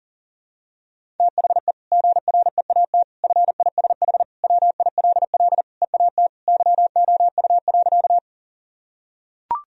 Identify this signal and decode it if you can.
Morse code, heard as a keyed tone, about 40 wpm, 700 Hz. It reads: THE GREAT FISH WILL EAT YOU.
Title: Sentences from Top 1000 Words - Review Part 1 - 40wpm